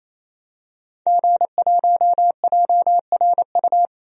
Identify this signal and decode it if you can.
Morse code, heard as a keyed tone, about 28 wpm, 700 Hz. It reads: G1JRU
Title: DXCC-100 Call Signs - 5 Characters - 28wpm